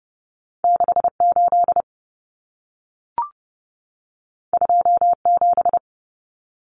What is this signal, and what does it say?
Morse code, heard as a keyed tone, about 30 wpm, 700 Hz. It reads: 68 27
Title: Numbers - 2-Digit 30wpm